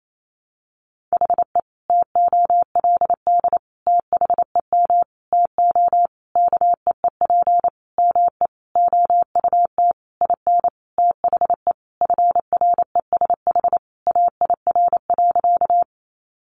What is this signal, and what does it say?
Morse code, heard as a keyed tone, about 28 wpm, 700 Hz. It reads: HE TOLD THEM TO KEEP ME OUT IN THE FRESH AIR.
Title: Sentences from Top 1200 Words 28wpm